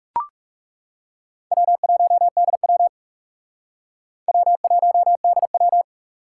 Morse code, heard as a keyed tone, about 40 wpm, 700 Hz. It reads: W1DW W1DW